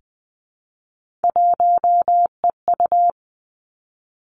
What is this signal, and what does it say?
Morse code, heard as a keyed tone, about 20 wpm, 700 Hz. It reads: 1EU
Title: Three Challenging Letter-Number Combinations 20wpm